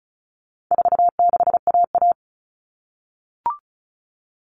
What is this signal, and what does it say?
Morse code, heard as a keyed tone, about 34 wpm, 700 Hz. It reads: 46AA
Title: All Cut Numbers - 4-Digit 35wpm